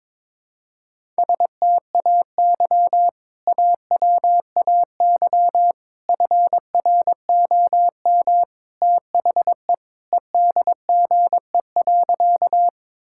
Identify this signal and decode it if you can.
Morse code, heard as a keyed tone, about 22 wpm, 700 Hz. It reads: STAY AWAY FROM THE EDGE.
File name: practice / 22wpm / Sentences with 3-Word Phrases - 22wpm